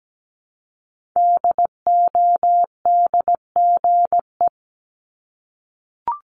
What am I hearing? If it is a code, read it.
Morse code, heard as a keyed tone, about 17 wpm, 700 Hz. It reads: DODGE
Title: Multi-letter Phonemes 17wpm